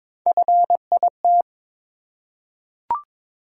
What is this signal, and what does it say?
Morse code, heard as a keyed tone, about 23 wpm, 700 Hz. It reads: FIT
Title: Three Letter Words - 22wpm